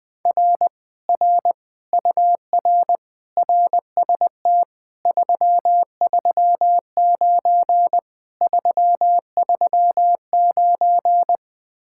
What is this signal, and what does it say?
Morse code, heard as a keyed tone, about 21 wpm, 700 Hz. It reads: R R UR RST 339 339
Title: Speed-Racing Common QSO Phrases Encore - 20wpm